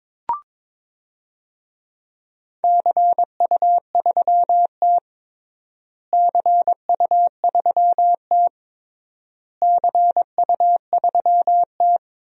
Morse code, heard as a keyed tone, about 22 wpm, 700 Hz. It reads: CU3T CU3T CU3T
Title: DXCC-100 Call Signs - 4 Characters - Repeated 3x - 22wpm